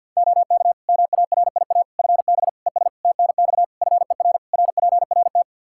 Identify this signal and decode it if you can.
Morse code, heard as a keyed tone, about 52 wpm, 700 Hz. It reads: OK MARIA FB ES TNX FER RPRT
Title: Practice QSO - Exchange 2 - Info - 50wpm